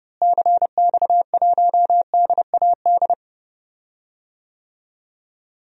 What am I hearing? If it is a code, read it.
Morse code, heard as a keyed tone, about 30 wpm, 700 Hz. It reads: CX1DAD
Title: DXCC-100 Call Signs - 6 Characters - Repeated 2x - 30wpm